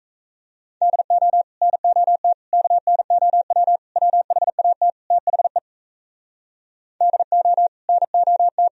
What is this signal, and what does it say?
Morse code, heard as a keyed tone, about 37 wpm, 700 Hz. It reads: DO NOT KNOW WHAT THE DO NOT